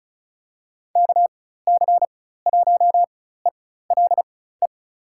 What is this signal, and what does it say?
Morse code, heard as a keyed tone, about 36 wpm, spaced as for 16 wpm, 700 Hz. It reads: KC1ELE
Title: USA Call Signs - 4x Character Spacing - 35wpm